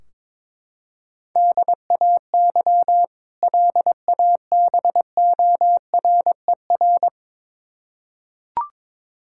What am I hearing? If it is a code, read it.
Morse code, heard as a keyed tone, about 22 wpm, 700 Hz. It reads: DAY LABORER
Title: K1USN SST QSO Practice 22wpm